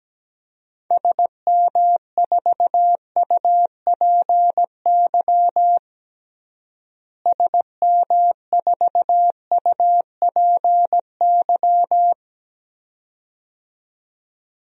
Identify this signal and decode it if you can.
Morse code, heard as a keyed tone, about 17 wpm, 700 Hz. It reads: SM4UPY SM4UPY
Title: DXCC-100 Call Signs - 6 Characters - Repeated 2x - 17wpm